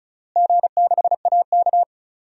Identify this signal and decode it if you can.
Morse code, heard as a keyed tone, about 34 wpm, 700 Hz. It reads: G6AK
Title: DXCC-100 Call Signs - 4 Characters - Repeated 3x - 35wpm